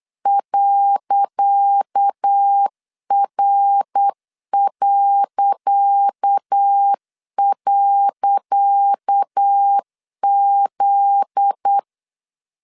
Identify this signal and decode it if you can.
Morse code, heard as a keyed tone, about 8 wpm, 800 Hz. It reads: .R..Z
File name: LEZIONI / 10 SIGNS / 10C DOT 8wpm 3649